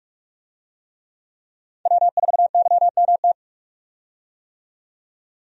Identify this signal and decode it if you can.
Morse code, heard as a keyed tone, about 43 wpm, 700 Hz. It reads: W4YMT
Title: USA Call Signs - 5 Characters - 45wpm